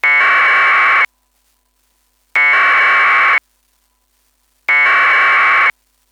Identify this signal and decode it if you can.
Packet radio (AX.25, as in APRS).
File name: aprs pico beacon demo